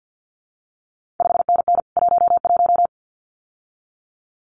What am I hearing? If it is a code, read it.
Morse code, heard as a keyed tone, about 50 wpm, 700 Hz. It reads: 5NN 11
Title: CQ WW DX Contest Exchange - 50wpm